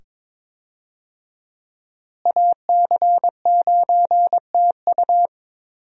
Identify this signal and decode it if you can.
Morse code, heard as a keyed tone, about 22 wpm, 700 Hz. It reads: AC9TU
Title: USA Call Signs - 5 Characters - Repeated 2x - 22wpm